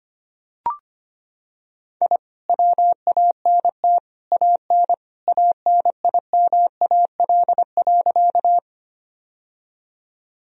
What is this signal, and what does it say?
Morse code, heard as a keyed tone, about 25 wpm, 700 Hz. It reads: I WANT AN ANIMAL.
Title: Sentences from Top 1000 Words - Review Part 1 - 25wpm